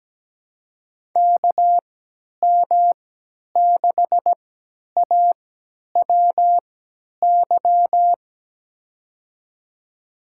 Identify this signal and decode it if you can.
Morse code, heard as a keyed tone, about 17 wpm, spaced as for 10 wpm, 700 Hz. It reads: KM6AWY
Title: USA Call Signs - 6 Characters - 3x Character Spacing - 17wpm